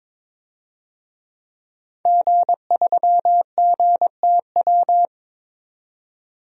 Morse code, heard as a keyed tone, about 22 wpm, 700 Hz. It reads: G3GTW
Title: DXCC-100 Call Signs - 5 Characters - 22wpm